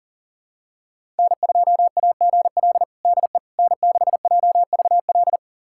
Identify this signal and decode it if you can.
Morse code, heard as a keyed tone, about 40 wpm, 700 Hz. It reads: N2AGL DE N6JVL
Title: USA Callsign DE Callsign - 5 Characters - 40wpm